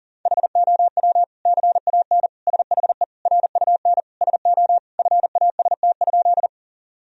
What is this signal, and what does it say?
Morse code, heard as a keyed tone, about 40 wpm, 700 Hz. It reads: HOW CAN SHE RUN SO FAST?
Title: Sentences from Top 1000 Words - Review Part 2 - 40wpm